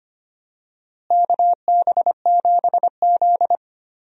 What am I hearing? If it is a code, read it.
Morse code, heard as a keyed tone, about 25 wpm, 700 Hz. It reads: KB7Z
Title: USA Call Signs - 4 Characters - Repeated 2x - 25wpm